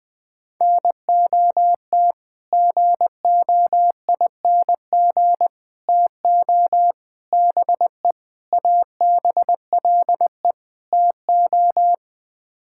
Morse code, heard as a keyed tone, about 20 wpm, 700 Hz. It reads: NOT GOING TO BE ABLE TO
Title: Sets of 6 Words - 20wpm